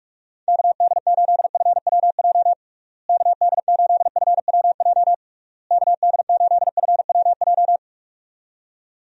Morse code, heard as a keyed tone, about 46 wpm, 700 Hz. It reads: KD8FWJ KD8FWJ KD8FWJ
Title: USA Call Signs - 6 Characters - Repeated 3x - 45wpm